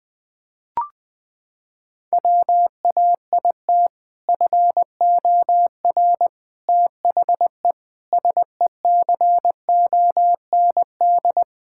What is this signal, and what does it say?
Morse code, heard as a keyed tone, about 19 wpm, 700 Hz. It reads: WAIT FOR THE SECOND
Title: Sentences with 2-Word Phrases - 20wpm